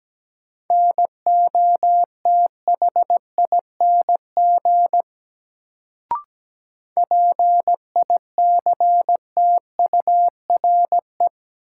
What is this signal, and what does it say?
Morse code, heard as a keyed tone, about 17 wpm, 700 Hz. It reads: NOTHING PICTURE
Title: Sentences from Top 500 Words - New Word Drill - 17wpm